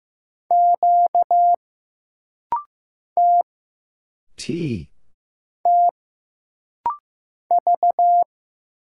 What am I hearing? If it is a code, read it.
Morse code, heard as a keyed tone, about 15 wpm, 700 Hz. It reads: Q T T V
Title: Single Letters - Rapid-Fire 15wpm